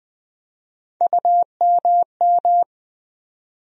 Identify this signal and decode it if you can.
Morse code, heard as a keyed tone, about 20 wpm, 700 Hz. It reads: UMM
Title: USA Call Sign Postfixes - 3 Letters 20wpm